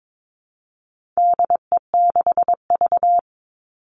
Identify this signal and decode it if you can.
Morse code, heard as a keyed tone, about 22 wpm, 700 Hz. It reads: DE6V
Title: All Cut Numbers - 4-Digit 22wpm